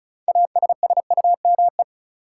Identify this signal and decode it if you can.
Morse code, heard as a keyed tone, about 36 wpm, 700 Hz. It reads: ASSUME
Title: Sentences from Top 1500 Words - New Word Drill - 35wpm